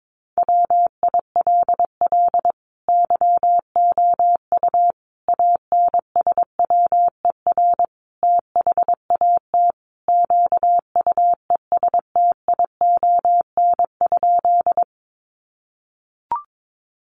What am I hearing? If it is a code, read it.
Morse code, heard as a keyed tone, about 22 wpm, 700 Hz. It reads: WILL YOU ANSWER THAT QUESTION?
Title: Sentences from Top 400 Words 22wpm